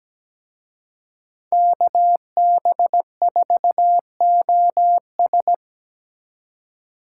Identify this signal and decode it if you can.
Morse code, heard as a keyed tone, about 20 wpm, 700 Hz. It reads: KB4OS
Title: Speed Racing - USA 2x2 Call Signs 17wpm